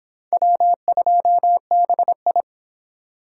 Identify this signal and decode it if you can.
Morse code, heard as a keyed tone, about 25 wpm, 700 Hz. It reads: W2BI